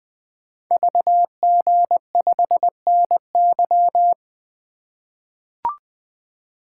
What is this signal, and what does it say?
Morse code, heard as a keyed tone, about 20 wpm, 700 Hz. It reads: VG5NY
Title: Code Groups - 5 Letter:Numbers - 20wpm